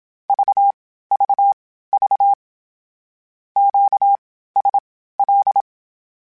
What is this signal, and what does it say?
Morse code, heard as a keyed tone, about 27 wpm, spaced as for 15 wpm, 800 Hz. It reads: VVV QSL
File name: (spaced) / Q-Calls Spaced-mp3s / Q Calls Twice sp @25